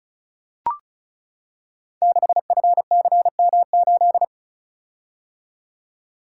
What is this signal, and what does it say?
Morse code, heard as a keyed tone, about 36 wpm, 700 Hz. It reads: 6FCM8
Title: Code Groups - 5 Letter:Numbers - 35wpm